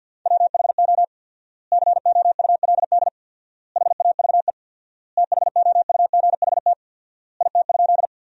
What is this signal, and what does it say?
Morse code, heard as a keyed tone, about 47 wpm, 700 Hz. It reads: WHO COULD HAVE THOUGHT IT?
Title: 4x Word-Spacing - Sentences from Top 300 Words - 50wpm